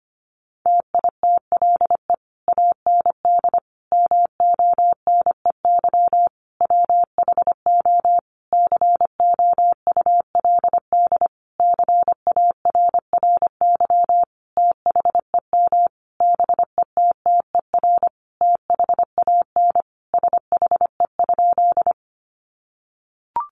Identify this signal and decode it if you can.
Morse code, heard as a keyed tone, about 25 wpm, 700 Hz. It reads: TITLE AND MONEY WHO COULD CARRY THEM BETTER THAN SHE?
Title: Sentences from Top 1200 Words 25wpm